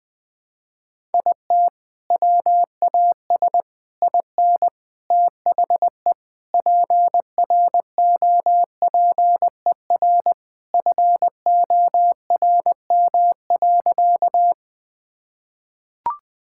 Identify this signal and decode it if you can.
Morse code, heard as a keyed tone, about 20 wpm, 700 Hz. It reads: IT WAS IN THE PROPER FORM.
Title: Sentences from Top 1000 Words - Review Part 5 - 20wpm